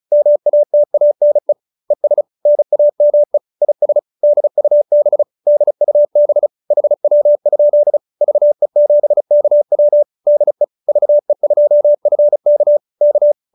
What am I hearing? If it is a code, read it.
Morse code, heard as a keyed tone, about 34 wpm, 590 Hz. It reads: MATANE ES NAME IS DUB DUB HW? VE7KW DE VE2FK K